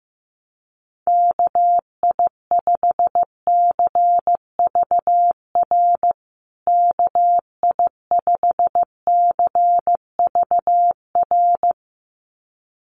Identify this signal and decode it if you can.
Morse code, heard as a keyed tone, about 15 wpm, 700 Hz. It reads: KI5CVR KI5CVR
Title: USA Call Signs Repeated 2x 15wpm